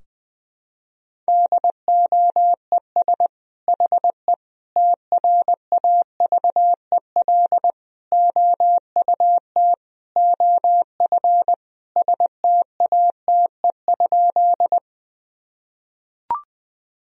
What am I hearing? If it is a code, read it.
Morse code, heard as a keyed tone, about 21 wpm, 700 Hz. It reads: DOES HE TRAVEL OUT OF STATE?
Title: Sentences from Top 1000 Words - Review Part 2 - 20wpm